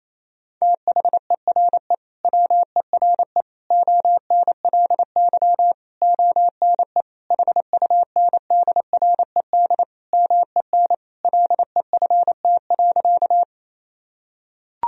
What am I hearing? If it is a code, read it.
Morse code, heard as a keyed tone, about 28 wpm, 700 Hz. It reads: THERE WERE ONLY ONE HUNDRED MEN LEFT.
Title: Sentences from Top 1000 Words - Review Part 2 - 28wpm